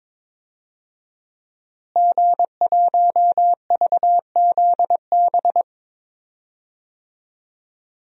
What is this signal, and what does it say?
Morse code, heard as a keyed tone, about 22 wpm, 700 Hz. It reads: G1VZB
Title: DXCC-100 Call Signs - 5 Characters - Repeated 3x - 22wpm